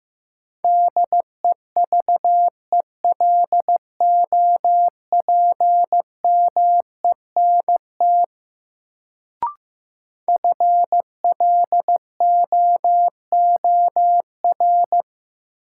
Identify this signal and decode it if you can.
Morse code, heard as a keyed tone, about 15 wpm, 700 Hz. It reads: DEVELOPMENT FLOOR
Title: Sets of 3 Words - New Word Drill - 15wpm